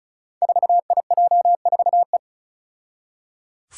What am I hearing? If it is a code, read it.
Morse code, heard as a keyed tone, about 36 wpm, 700 Hz. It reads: 4IJ4E
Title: Five Challenging Letter-Number Combinations 35wpm